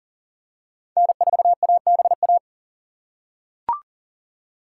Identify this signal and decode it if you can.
Morse code, heard as a keyed tone, about 40 wpm, 700 Hz. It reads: N4ABA
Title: USA Call Signs - 5 Characters - 2x Character Spacing - 40wpm